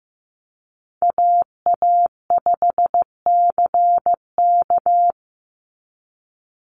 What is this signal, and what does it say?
Morse code, heard as a keyed tone, about 15 wpm, 700 Hz. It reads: AA5CK
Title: NA SOTA Chaser Callsigns 15wpm